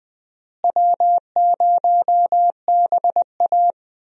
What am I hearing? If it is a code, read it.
Morse code, heard as a keyed tone, about 20 wpm, 700 Hz. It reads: W0BA
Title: USA Call Signs - 4 Characters - Repeated 2x - 20wpm